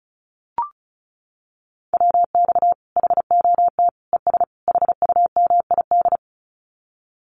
Morse code, heard as a keyed tone, about 34 wpm, 700 Hz. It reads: WX HOT ES HUMID
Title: Common QSO Phrases 35wpm